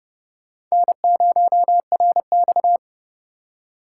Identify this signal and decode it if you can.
Morse code, heard as a keyed tone, about 28 wpm, 700 Hz. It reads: N0RX